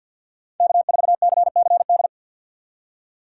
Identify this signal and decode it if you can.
Morse code, heard as a keyed tone, about 51 wpm, 700 Hz. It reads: K4CCD
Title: USA Call Signs - 5 Characters - 2x Character Spacing - 50wpm